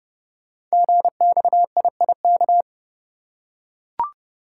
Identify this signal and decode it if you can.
Morse code, heard as a keyed tone, about 31 wpm, 700 Hz. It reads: GXIIK
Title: Code Groups - 5 Letters - 30wpm